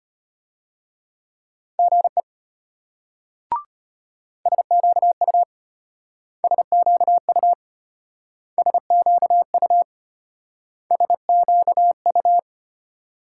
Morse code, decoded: GE SQU SQU SQU SQU

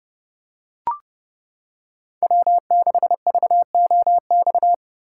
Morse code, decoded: W6VOX

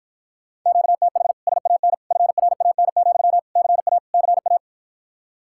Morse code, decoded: QTH SAN FRAN, CA CA

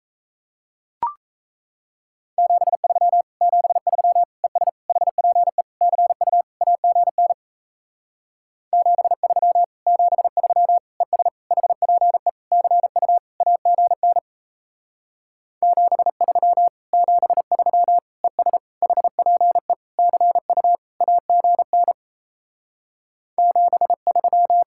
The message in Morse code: 73 73 ES HPE CU AGN 73 73 ES HPE CU AGN 73 73 ES HPE CU AGN 73

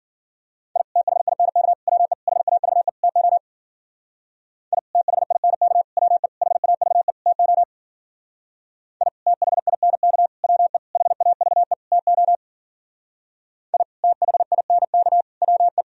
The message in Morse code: I THINK WE HAVE TO I THINK WE HAVE TO I THINK WE HAVE TO I THINK WE

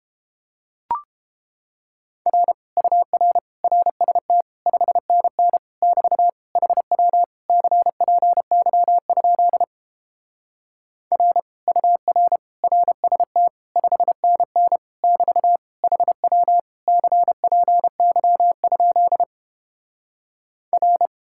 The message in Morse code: R UR RST 5NN <BT> HW CPY? R UR RST 5NN <BT> HW CPY? R

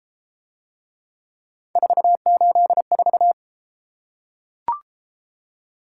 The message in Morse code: 484

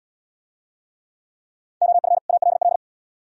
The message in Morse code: OF ALL

